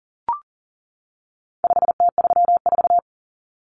Morse code, heard as a keyed tone, about 40 wpm, 700 Hz. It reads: 5T34
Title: Common Cut Numbers - 4-Digit 40wpm